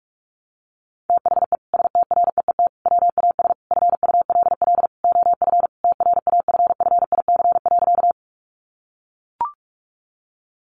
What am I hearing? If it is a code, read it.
Morse code, heard as a keyed tone, about 44 wpm, 700 Hz. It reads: THE STREET WAS FULL OF TRAFFIC.
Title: Sentences from Top 1300 Words 45wpm